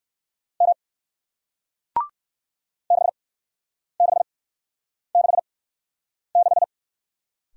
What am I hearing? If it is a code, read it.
Morse code, heard as a keyed tone, about 56 wpm, spaced as for 9 wpm, 700 Hz. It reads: M 6666
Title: Speed-Racing Single Letter-Number 45wpm